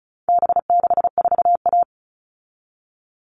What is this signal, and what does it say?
Morse code, heard as a keyed tone, about 34 wpm, 700 Hz. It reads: B64A
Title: All Cut Numbers - 4-Digit 35wpm